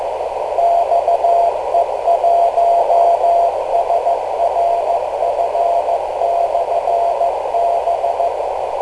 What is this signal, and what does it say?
Morse code, heard as a keyed tone, about 15 wpm, 690 Hz, with noise. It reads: XE1SRF/B
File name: kl7fdq / beacons / xe1srf-rst439